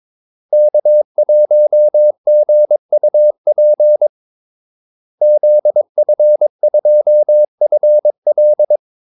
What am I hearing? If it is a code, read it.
Morse code, heard as a keyed tone, about 22 wpm, 600 Hz. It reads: K1GUP ZF2FL
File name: Callsigns-22wpm